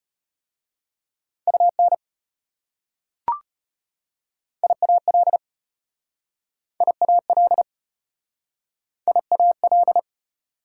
Morse code, heard as a keyed tone, about 34 wpm, 690 Hz. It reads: UN IAL IAL IAL